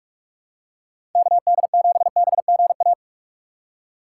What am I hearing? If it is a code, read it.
Morse code, heard as a keyed tone, about 45 wpm, 700 Hz. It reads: KD7BGA
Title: USA Call Signs - 2x Character Spacing - 45wpm